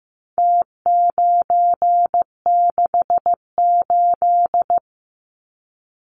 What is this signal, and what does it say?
Morse code, heard as a keyed tone, about 15 wpm, 700 Hz. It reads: T968
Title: Numbers - Contest Style - 4-Digit 15wpm